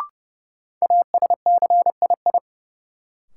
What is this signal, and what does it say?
Morse code, heard as a keyed tone, about 30 wpm, 700 Hz. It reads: ASCII
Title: Sets of 5 Words - New Word Drill - 30wpm